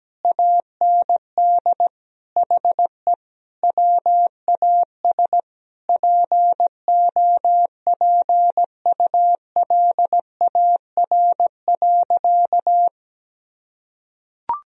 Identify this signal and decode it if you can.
Morse code, heard as a keyed tone, about 17 wpm, 700 Hz. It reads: AND HE WAS POPULAR.
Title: Sentences from Top 1000 Words - Review Part 3 - 17wpm